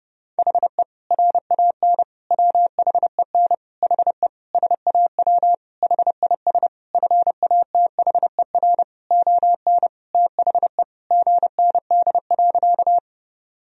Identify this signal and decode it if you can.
Morse code, heard as a keyed tone, about 32 wpm, 700 Hz. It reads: HE RAN WHEN HE SAW HIS FATHER ON THE GND.
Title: Sentences from Top 500 Words and Abbreviations 30wpm